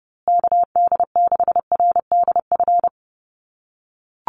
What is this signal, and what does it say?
Morse code, heard as a keyed tone, about 28 wpm, 700 Hz. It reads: KD6RDF
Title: USA Call Signs Repeated 3x 30wpm